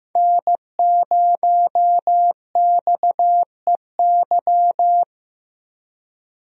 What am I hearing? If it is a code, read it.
Morse code, heard as a keyed tone, about 15 wpm, 700 Hz. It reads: N0XEY
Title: USA Call Signs - 5 Characters - 15wpm